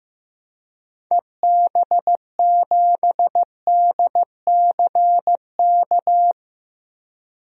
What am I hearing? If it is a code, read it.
Morse code, heard as a keyed tone, about 15 wpm, 700 Hz. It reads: EB7DCK
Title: DXCC-100 Call Signs - 6 Characters - 15wpm